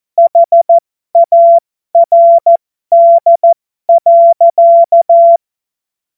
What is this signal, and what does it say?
Morse code, heard as a keyed tone, about 14 wpm, spaced as for 12 wpm, 680 Hz. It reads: HARD.
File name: Morsekurs / Witze